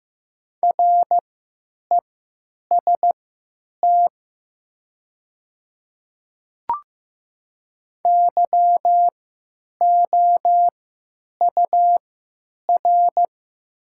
Morse code, heard as a keyed tone, about 15 wpm, spaced as for 9 wpm, 700 Hz. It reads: REST YOUR
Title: 3x Character Spacing - Top 500 Words - 15wpm